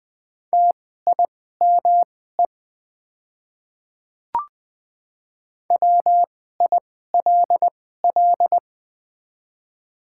Morse code, heard as a keyed tone, about 20 wpm, spaced as for 15 wpm, 700 Hz. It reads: TIME WILL
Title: 2x Character Spacing - Top 100 Words - 20wpm